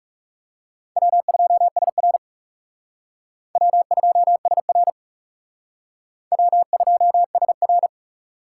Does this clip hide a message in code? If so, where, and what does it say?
Yes: Morse code from 1.0 to 7.9 s, heard as a keyed tone, about 38 wpm, 700 Hz. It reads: W2SR W2SR W2SR